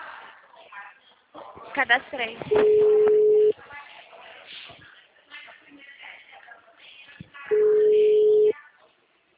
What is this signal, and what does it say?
Ringback tone (Europe, 425 Hz), with speech.